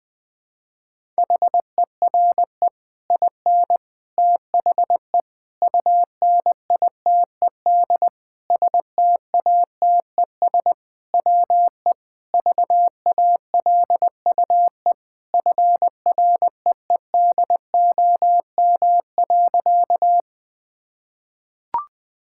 Morse code, heard as a keyed tone, about 21 wpm, 700 Hz. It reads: HERE IN THE UNITED STATES WE VALUE FREEDOM.
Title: Sentences with 5-Word Phrases - Spoken Beginning - 20wpm